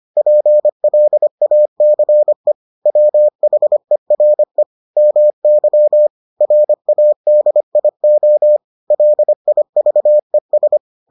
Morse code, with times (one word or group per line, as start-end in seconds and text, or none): 0.2-2.5 PLACE
2.9-4.6 WHERE
5.0-6.1 MY
6.4-8.6 RADIO
8.9-10.8 LIVES